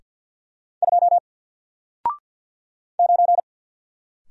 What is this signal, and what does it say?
Morse code, heard as a keyed tone, about 48 wpm, 700 Hz. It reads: 2 9
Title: Single Character (Letter:Number:Punctuation:Prosign) 50wpm